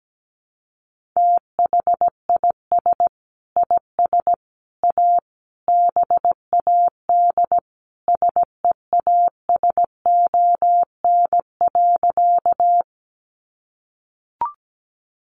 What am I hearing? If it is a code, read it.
Morse code, heard as a keyed tone, about 17 wpm, 700 Hz. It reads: THIS IS A BAD SEASON.
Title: Sentences from Top 1400 Words 17wpm